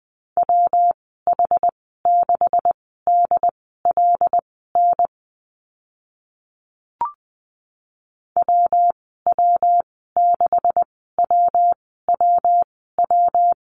Morse code, heard as a keyed tone, about 20 wpm, spaced as for 14 wpm, 700 Hz. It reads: WH6DLN WW6WWW
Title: 140 Lesson 23g - Letter f - Call Signs 20wpm